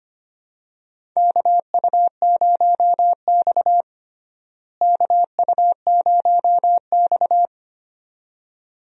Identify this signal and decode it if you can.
Morse code, heard as a keyed tone, about 25 wpm, 700 Hz. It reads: KU0X KU0X
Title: USA Call Signs - 4 Characters - Repeated 2x - 25wpm